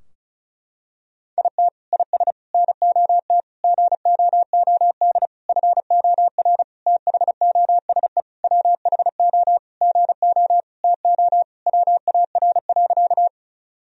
Morse code, heard as a keyed tone, about 36 wpm, 700 Hz. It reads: IT IS NOT GOOD FOR THOSE WHO GO TO WAR.
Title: Sentences from Top 1000 Words - Review Part 2 - 35wpm